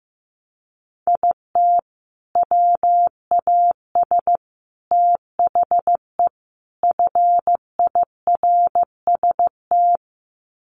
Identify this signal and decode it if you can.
Morse code, heard as a keyed tone, about 15 wpm, 700 Hz. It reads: IT WAS THE FIRST
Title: Sets of 4 Words 15wpm